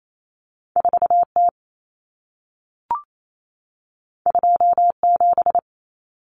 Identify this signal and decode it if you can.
Morse code, heard as a keyed tone, about 28 wpm, 700 Hz. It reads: 4T 27
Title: Common Cut Numbers - 2-Digit 28wpm